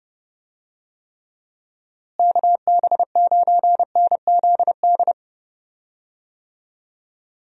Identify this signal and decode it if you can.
Morse code, heard as a keyed tone, about 32 wpm, 700 Hz. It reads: KB9NZD
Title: USA Call Signs - 6 Characters - Repeated 2x - 30wpm